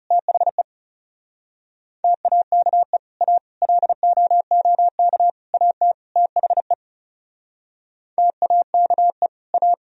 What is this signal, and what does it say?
Morse code, heard as a keyed tone, about 36 wpm, 700 Hz. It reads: THE TAKE A LOOK AT THE TAKE A
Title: Speed-Racing - Sets of 5 Words 30wpm